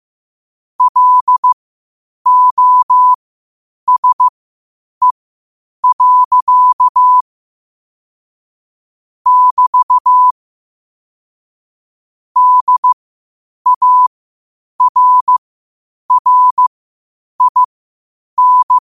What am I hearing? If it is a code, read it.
Morse code, heard as a keyed tone, about 15 wpm, spaced as for 8 wpm, 1000 Hz. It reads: LOSE. <BT> DARRIN
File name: 8wpm